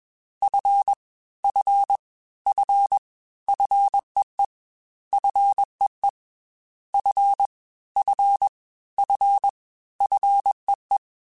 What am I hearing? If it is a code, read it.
Morse code, heard as a keyed tone, about 21 wpm, 800 Hz. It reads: F F F FEE FEE F F F FEE
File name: Fs / Single F mp3s / Single F Prefixes 3x2 Twice @20